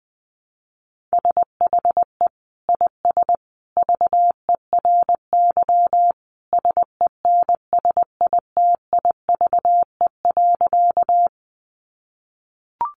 Morse code, heard as a keyed tone, about 20 wpm, 700 Hz. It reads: SHE IS VERY SENSITIVE.
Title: Sentences from Top 1400 Words 20wpm